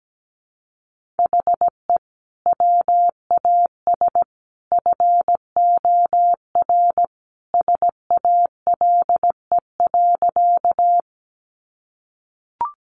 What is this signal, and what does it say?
Morse code, heard as a keyed tone, about 17 wpm, 700 Hz. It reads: HE WAS FOR SALE.
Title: Sentences from Top 1400 Words 17wpm